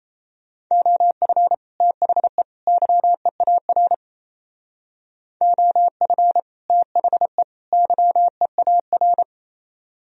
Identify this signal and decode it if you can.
Morse code, heard as a keyed tone, about 37 wpm, 700 Hz. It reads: OF THE YEAR OF THE YEAR